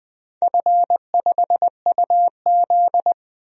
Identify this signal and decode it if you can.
Morse code, heard as a keyed tone, about 20 wpm, 700 Hz. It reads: F5UZ
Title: DXCC-100 Call Signs - 4 Characters - 20wpm